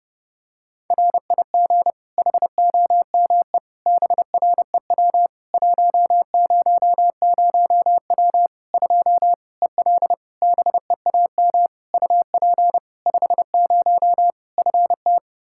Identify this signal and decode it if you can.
Morse code, heard as a keyed tone, about 30 wpm, 700 Hz. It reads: RIG HOME BREW 100W 2 EL BEAM UP 50 FT
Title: Practice QSO - Exchange 2 - Info - 30wpm